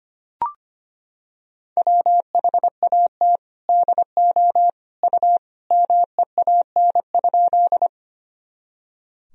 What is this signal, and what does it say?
Morse code, heard as a keyed tone, about 25 wpm, 700 Hz. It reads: WHAT DO U MEAN?
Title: Sentences from Top 200 Words and Abbreviations 25wpm